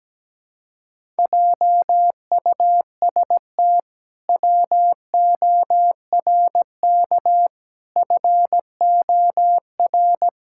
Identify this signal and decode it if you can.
Morse code, heard as a keyed tone, about 17 wpm, 700 Hz. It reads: JUST WORK FOR
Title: Pangram Challenge v2 - 17wpm